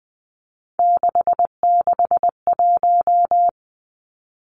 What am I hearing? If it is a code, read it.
Morse code, heard as a keyed tone, about 20 wpm, 700 Hz. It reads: 661
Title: Three Challenging Letter-Number Combinations 20wpm